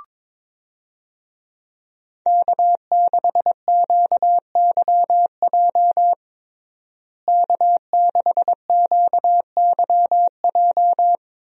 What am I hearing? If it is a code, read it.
Morse code, heard as a keyed tone, about 21 wpm, 700 Hz. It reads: K6QYJ K6QYJ
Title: USA Call Signs - 5 Characters - Repeated 3x - 22wpm